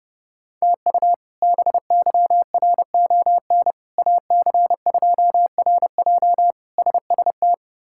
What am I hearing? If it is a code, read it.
Morse code, heard as a keyed tone, about 30 wpm, 700 Hz. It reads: TU BYRON AC2RJ SST
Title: K1USN SST QSO Practice - 30wpm